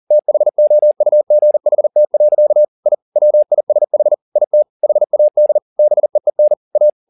Morse code, heard as a keyed tone, about 40 wpm, 600 Hz. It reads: THOUGHT. I WISH IT HAD BEEN A
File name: the old man and the sea 2 40wpm